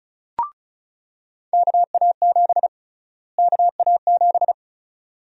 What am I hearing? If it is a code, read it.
Morse code, heard as a keyed tone, about 36 wpm, 700 Hz. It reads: KA7 KA7
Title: USA Call Sign Prefixes - Repeated 2x - 35wpm